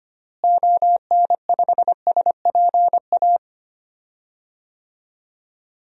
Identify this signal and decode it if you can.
Morse code, heard as a keyed tone, about 25 wpm, 700 Hz. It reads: ON5SPA
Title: DXCC-100 Call Signs - 6 Characters - Repeated 2x - 25wpm